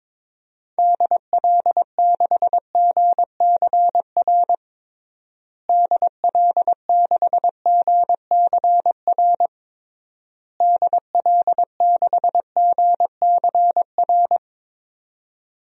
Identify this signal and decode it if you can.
Morse code, heard as a keyed tone, about 21 wpm, 700 Hz. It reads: DL6GCR DL6GCR DL6GCR
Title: DXCC-100 Call Signs - 6 Characters - Repeated 3x - 22wpm